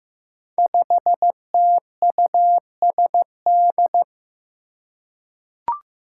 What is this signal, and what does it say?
Morse code, heard as a keyed tone, about 15 wpm, 700 Hz. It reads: 5TUSD
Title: Five Challenging Letter-Number Combinations 15wpm